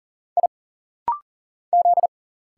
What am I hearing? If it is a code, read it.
Morse code, heard as a keyed tone, about 40 wpm, 700 Hz. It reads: I Z
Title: Single Letters - Rapid-Fire 40wpm